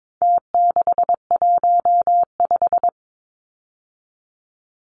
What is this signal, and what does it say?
Morse code, heard as a keyed tone, about 22 wpm, 700 Hz. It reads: T615
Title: Numbers - Contest Style - 4-Digit 22wpm